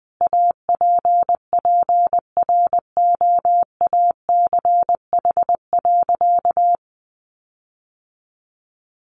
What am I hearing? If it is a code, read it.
Morse code, heard as a keyed tone, about 20 wpm, 700 Hz. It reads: APPROACH.
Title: Sentences from Top 1400 Words 20wpm